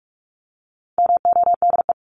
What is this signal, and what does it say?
Morse code, heard as a keyed tone, about 44 wpm, 700 Hz. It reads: MODE
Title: Top 1751-2000 Words 45wpm